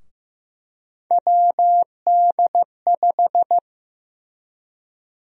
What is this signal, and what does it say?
Morse code, heard as a keyed tone, about 15 wpm, 700 Hz. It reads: WD5
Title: USA Call Sign Prefixes - Repeated 2x - 15wpm